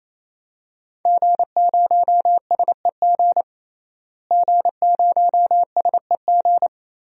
Morse code, heard as a keyed tone, about 28 wpm, 700 Hz. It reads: G0SEG G0SEG
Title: DXCC-100 Call Signs - 5 Characters - Repeated 3x - 28wpm